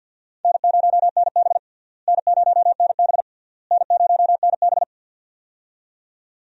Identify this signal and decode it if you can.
Morse code, heard as a keyed tone, about 47 wpm, 700 Hz. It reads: N0NB N0NB N0NB
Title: USA Call Signs - 4 Characters - Repeated 3x - 50wpm